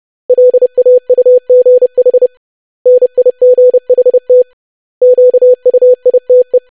Morse code, heard as a keyed tone, about 30 wpm, 500 Hz. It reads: LAUGH NIGHT QUITE
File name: lesson / web / basic500-25wpm